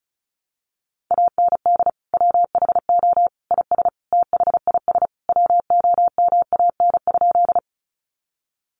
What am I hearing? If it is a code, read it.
Morse code, heard as a keyed tone, about 34 wpm, 700 Hz. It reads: AND WHO IS THIS WOMAN?
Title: Sentences from Top 1100 Words 35wpm